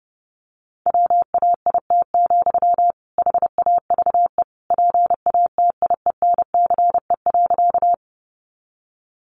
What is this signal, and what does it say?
Morse code, heard as a keyed tone, about 30 wpm, 700 Hz. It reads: WAIT, HAVE PATIENCE.
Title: Sentences from Top 1300 Words 30wpm